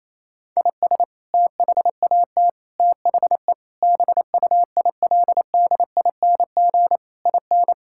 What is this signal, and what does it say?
Morse code, heard as a keyed tone, about 28 wpm, 700 Hz. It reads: IS THAT THE BUILDING IN
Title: Sentences from Top 1000 Words - Review Part 5 - 28wpm